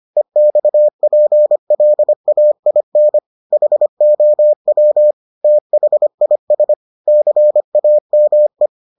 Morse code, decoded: EXPLAIN HOW THIS CAME

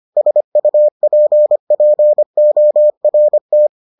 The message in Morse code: SUPPORT

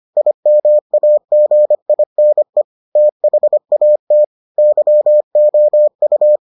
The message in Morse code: IMAGINE THAT YOU

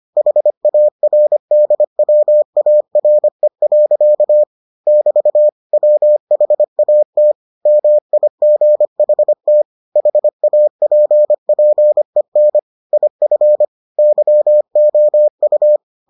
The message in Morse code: HARDWARE. <BT> WHAT MIGHT HAPPEN IF YOU

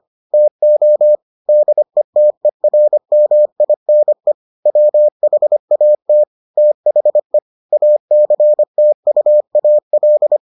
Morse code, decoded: TO DETERMINE WHAT THE ACTUAL

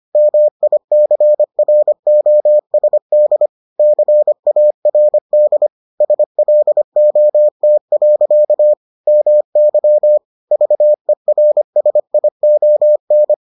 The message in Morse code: MICROSD CARD SLOT. MY VERSION

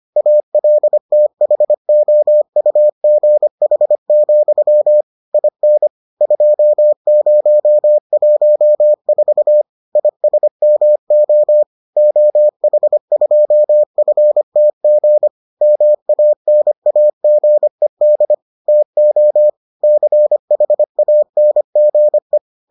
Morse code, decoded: ALTHOUGH, IN 2014 ISMO OH2FTG MANAGED TO CHANGE

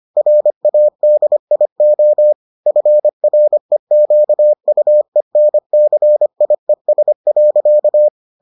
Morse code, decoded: RADIO FREQUENCIES.

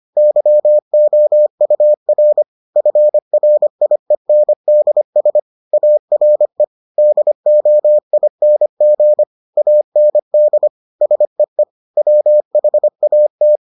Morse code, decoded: YOUR FRIENDS ARE DOING AND SEE WHAT

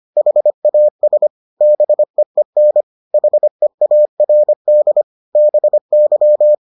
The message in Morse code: HAS BEEN HEARD BY

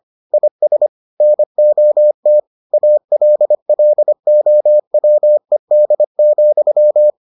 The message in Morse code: IS NOT ALLOWED,

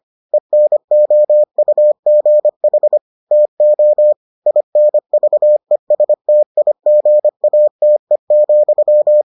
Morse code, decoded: ENOUGH TO INVESTIGATE,

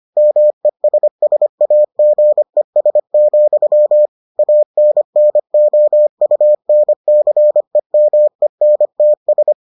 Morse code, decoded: MESSAGES, ANNOUNCEMENTS

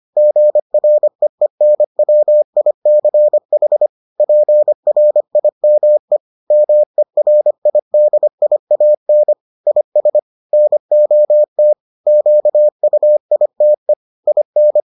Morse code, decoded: GREENWICH PRIME MERIDIAN IS NOT QUITE IN

